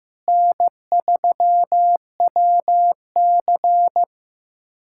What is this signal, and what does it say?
Morse code, heard as a keyed tone, about 15 wpm, 700 Hz. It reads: N3WC
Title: USA Callsign DE Callsign - 4 Characters - 15wpm